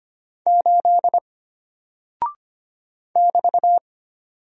Morse code, heard as a keyed tone, about 25 wpm, 700 Hz. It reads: 8 <BT>
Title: Single Character (Letter:Number:Punctuation:Prosign) Alt 25wpm